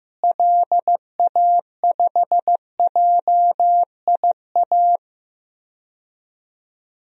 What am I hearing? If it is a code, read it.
Morse code, heard as a keyed tone, about 15 wpm, 700 Hz. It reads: LA5JIA
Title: DXCC-100 Call Signs - 6 Characters - Repeated 2x - 15wpm